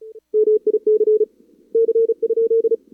Morse code, heard as a keyed tone, about 33 wpm, 440 Hz. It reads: MIC C?
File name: CWSprint / Feb2022 / k5tr-feb-na-sprint-cw-k5ot -0000